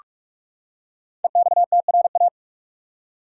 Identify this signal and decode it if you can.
Morse code, heard as a keyed tone, about 44 wpm, 700 Hz. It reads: EXTRA
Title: Five Letter Words - 45wpm